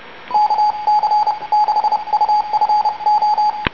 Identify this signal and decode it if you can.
Morse code, heard as a keyed tone, about 30 wpm, spaced as for 26 wpm, 840 Hz. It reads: KC6UFO